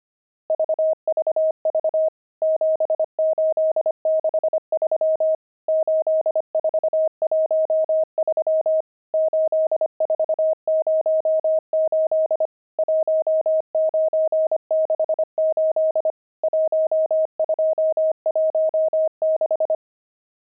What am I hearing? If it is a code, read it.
Morse code, heard as a keyed tone, about 25 wpm, 630 Hz. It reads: VVV 7863 8413 8408 1968 1216
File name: pangram-fs-25-cs-25-nums1